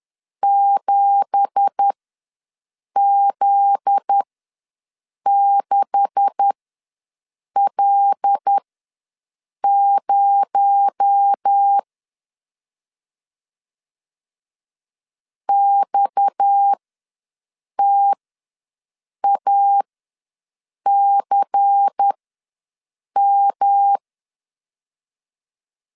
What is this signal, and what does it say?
Morse code, heard as a keyed tone, about 10 wpm, spaced as for 6 wpm, 800 Hz. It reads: 7Z6L0 XTACM
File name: LEZIONI / 10 SIGNS / 10B INT ws 10wpm 4134